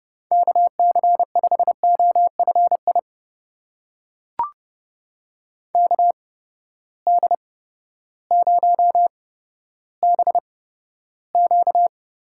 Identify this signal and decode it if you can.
Morse code, heard as a keyed tone, about 30 wpm, spaced as for 14 wpm, 700 Hz. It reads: KC5OFI KD0BQ
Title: USA Call Signs - 6 Characters - 8x Character Spacing - 30wpm